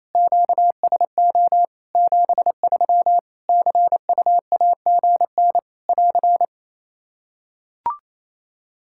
Morse code, heard as a keyed tone, about 28 wpm, 700 Hz. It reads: QSO 73 CUAGN <AR>
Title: Practice QSO - Exchange 4 - Ending - 28wpm